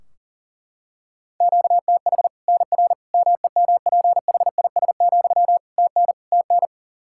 Morse code, heard as a keyed tone, about 40 wpm, 700 Hz. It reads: QTH NR MEMPHIS, TN TN
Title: Practice QSO - Exchange 1 - Beginning - 40wpm